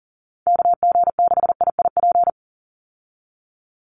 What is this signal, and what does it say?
Morse code, heard as a keyed tone, about 40 wpm, 700 Hz. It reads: KG6IIP
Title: Call Signs - 40wpm